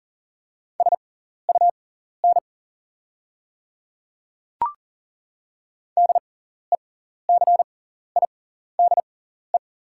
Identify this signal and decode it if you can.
Morse code, heard as a keyed tone, about 40 wpm, spaced as for 14 wpm, 690 Hz. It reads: SUN DECIDE
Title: 6x Character Spacing - Top 500 Words - 40wpm